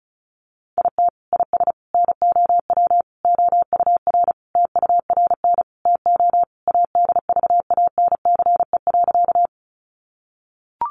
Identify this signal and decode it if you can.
Morse code, heard as a keyed tone, about 34 wpm, 700 Hz. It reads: IT IS NOW OUR TURN TO ADVANCE.
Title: Sentences from Top 1400 Words 35wpm